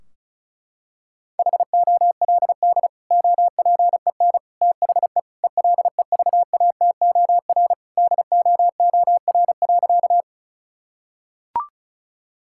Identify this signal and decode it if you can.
Morse code, heard as a keyed tone, about 36 wpm, 700 Hz. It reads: HOLD OPEN THE ELEVATOR DOOR.
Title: Sentences from Top 1600 Words 35wpm